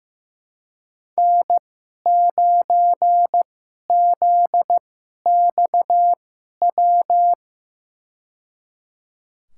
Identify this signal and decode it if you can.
Morse code, heard as a keyed tone, about 15 wpm, spaced as for 11 wpm, 700 Hz. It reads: N9ZXW
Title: USA Call Signs - 5 Characters - 2x Character Spacing - 15wpm